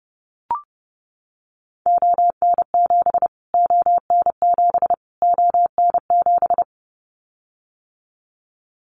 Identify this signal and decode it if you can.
Morse code, heard as a keyed tone, about 30 wpm, 700 Hz. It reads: ON7 ON7 ON7
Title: Callsign Prefixes for Top 25 DXCC - Repeated 3x - 30wpm